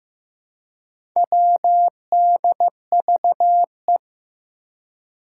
Morse code, heard as a keyed tone, about 15 wpm, 700 Hz. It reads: WDVE
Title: Four Challenging Letter-Number Combinations 15wpm